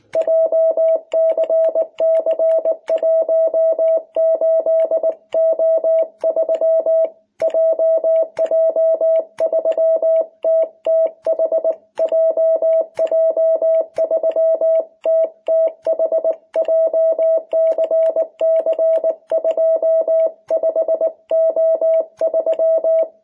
Morse code, heard as a keyed tone, about 19 wpm, 660 Hz. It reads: JCC18O3 JJ3TTH JJ3TTH JCC25O3
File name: jh8cbh / 2021cw